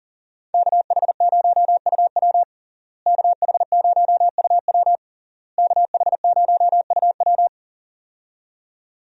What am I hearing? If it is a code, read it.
Morse code, heard as a keyed tone, about 40 wpm, 700 Hz. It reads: KH0UW KH0UW KH0UW
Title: USA Call Signs - 5 Characters - Repeated 3x - 40wpm